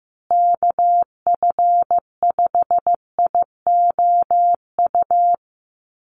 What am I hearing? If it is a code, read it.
Morse code, heard as a keyed tone, about 15 wpm, 700 Hz. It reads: KF5IOU
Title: USA Call Signs Repeated 2x 15wpm